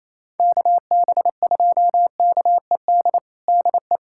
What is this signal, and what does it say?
Morse code, heard as a keyed tone, about 28 wpm, 700 Hz. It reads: KB2KED DE
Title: USA Callsign DE Callsign 28wpm